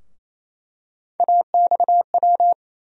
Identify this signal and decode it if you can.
Morse code, heard as a keyed tone, about 28 wpm, 700 Hz. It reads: AXW
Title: USA Call Sign Postfixes - 3 Letters 28wpm